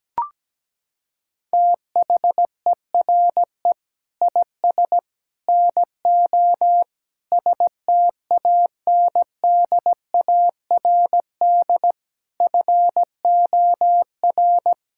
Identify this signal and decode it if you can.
Morse code, heard as a keyed tone, about 17 wpm, 700 Hz. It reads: THERE IS NO STANDARD FOR
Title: Sentences from Top 1600 Words 17wpm